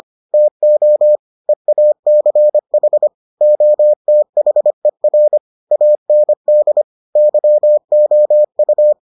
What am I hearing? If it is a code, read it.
Morse code, heard as a keyed tone, about 25 wpm, 600 Hz. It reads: TO EACH OTHER AND YOU